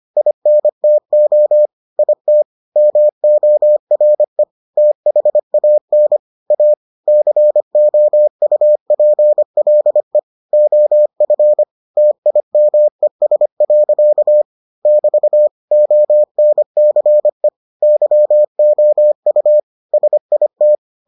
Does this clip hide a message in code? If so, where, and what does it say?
Yes: Morse code from 0.2 to 20.8 s, heard as a keyed tone, about 25 wpm, 600 Hz. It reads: INTO IT MORE THAN A COUPLE OF TIMES. <BT> ONCE YOU SIT